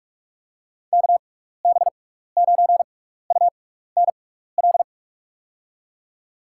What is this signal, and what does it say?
Morse code, heard as a keyed tone, about 44 wpm, spaced as for 15 wpm, 700 Hz. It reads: KB9UNL
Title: USA Call Signs - 6 Characters - 6x Character Spacing - 45wpm